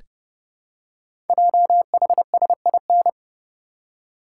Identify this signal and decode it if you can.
Morse code, heard as a keyed tone, about 29 wpm, 690 Hz. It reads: JHSIN
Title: Code Groups - 5 Letters - 30wpm